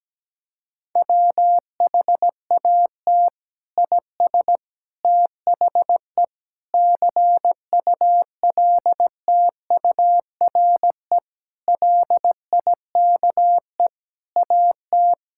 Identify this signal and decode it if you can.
Morse code, heard as a keyed tone, about 17 wpm, 700 Hz. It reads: WHAT IS THE CULTURE LIKE AT
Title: Sentences from Top 1000 Words - Review Part 6 - 17wpm